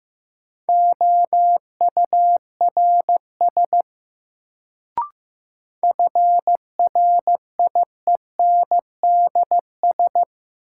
Morse code, heard as a keyed tone, about 15 wpm, 700 Hz. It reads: OURS FRIENDS
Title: Sentences from Top 1000 Words - Review Part 5 - New Word Drill - 15wpm